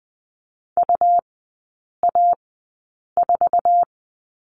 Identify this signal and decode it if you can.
Morse code, heard as a keyed tone, about 20 wpm, spaced as for 8 wpm, 700 Hz. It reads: UA4
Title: 184 Lesson 30e - Letter b - Three Characters 20wpm